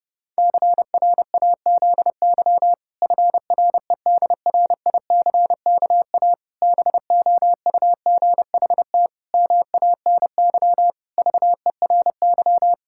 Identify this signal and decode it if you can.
Morse code, heard as a keyed tone, about 30 wpm, 700 Hz. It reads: CRAZY FREDRICKA BOUGHT MANY VERY
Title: Pangram Challenge v2 - 30wpm